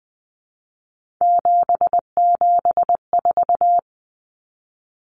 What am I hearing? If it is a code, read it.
Morse code, heard as a keyed tone, about 20 wpm, 700 Hz. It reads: 774
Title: Common Cut Numbers - 3-Digit 20wpm